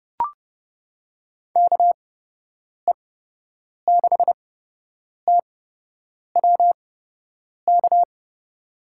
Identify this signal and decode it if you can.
Morse code, heard as a keyed tone, about 30 wpm, spaced as for 8 wpm, 700 Hz. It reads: KE6TWK
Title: USA Call Signs - 6 Characters - 8x Character Spacing - 30wpm